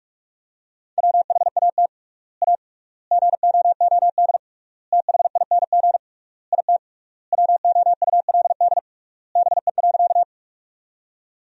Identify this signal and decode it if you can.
Morse code, heard as a keyed tone, about 46 wpm, 700 Hz. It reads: WHAT A GOOD THING IT WOULD BE.
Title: 3x Word-Spacing - Sentences from Top 200 Words - 45wpm